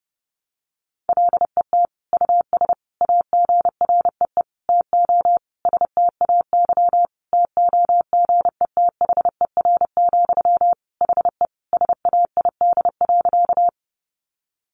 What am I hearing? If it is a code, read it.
Morse code, heard as a keyed tone, about 30 wpm, 700 Hz. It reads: LET US AGREE TO STAY TOGETHER, HE SAID.
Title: Sentences from Top 1100 Words 30wpm